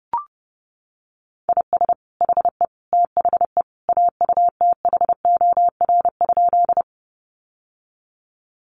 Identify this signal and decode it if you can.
Morse code, heard as a keyed tone, about 30 wpm, 700 Hz. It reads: IS HE THE AUTHOR?
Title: Sentences from Top 1200 Words 30wpm